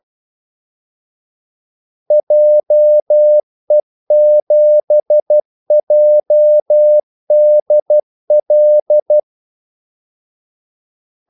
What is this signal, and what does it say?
Morse code, heard as a keyed tone, about 12 wpm, 600 Hz. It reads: JE7JDL
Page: Kaldesignaler 12 wpm